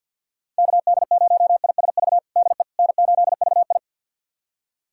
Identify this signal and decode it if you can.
Morse code, heard as a keyed tone, about 50 wpm, 700 Hz. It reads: KD0ISV DE N8VI
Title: USA Callsign DE Callsign 50wpm